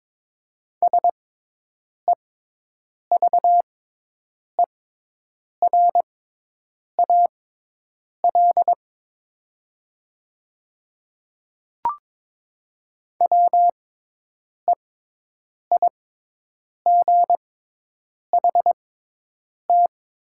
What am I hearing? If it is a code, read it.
Morse code, heard as a keyed tone, about 22 wpm, spaced as for 8 wpm, 700 Hz. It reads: SEVERAL WEIGHT
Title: 6x Character Spacing - Top 500 Words - 22wpm